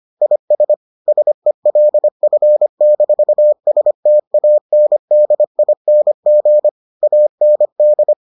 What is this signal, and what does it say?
Morse code, heard as a keyed tone, about 25 wpm, 600 Hz. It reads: IS SELF-STANDING AND